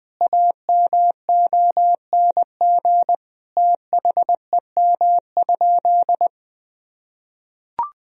Morse code, heard as a keyed tone, about 20 wpm, 700 Hz. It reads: AMONG THEM?
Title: Sentences from Top 1000 Words - Review Part 3 - 20wpm